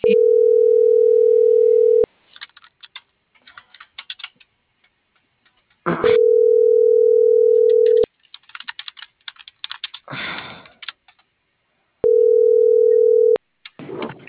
Ringback tone (North America), 440 + 480 Hz.